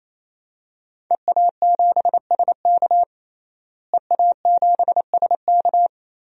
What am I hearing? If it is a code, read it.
Morse code, heard as a keyed tone, about 28 wpm, 690 Hz. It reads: EA7SK EA7SK
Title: DXCC-100 Call Signs - 5 Characters - Repeated 3x - 28wpm